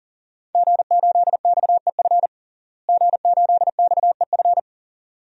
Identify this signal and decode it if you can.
Morse code, heard as a keyed tone, about 40 wpm, 700 Hz. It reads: G8XEF G8XEF
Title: DXCC-100 Call Signs - 5 Characters - Repeated 2x - 40wpm